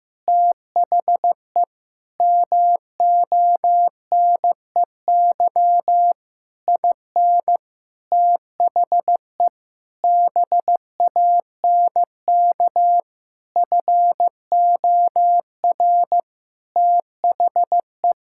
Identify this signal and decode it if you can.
Morse code, heard as a keyed tone, about 15 wpm, 700 Hz. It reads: THE MONEY IN THE BANK FOR THE
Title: Sentences from Top 1000 Words - Review Part 6 - 15wpm